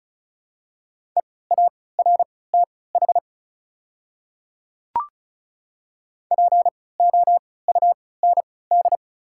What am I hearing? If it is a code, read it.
Morse code, heard as a keyed tone, about 36 wpm, spaced as for 20 wpm, 690 Hz. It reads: EARTH POUND
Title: 3x Character Spacing - Top 500 Words - 35wpm